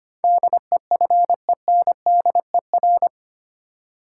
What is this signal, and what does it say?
Morse code, heard as a keyed tone, about 25 wpm, 700 Hz. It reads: DEFENDER
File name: Eight Letter Words - 25wpm